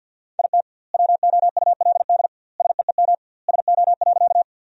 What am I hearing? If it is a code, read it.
Morse code, heard as a keyed tone, about 50 wpm, 700 Hz. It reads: IT WOULD SEEM SO.
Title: 2x Word-Spacing - Sentences from Top 300 Words - 50wpm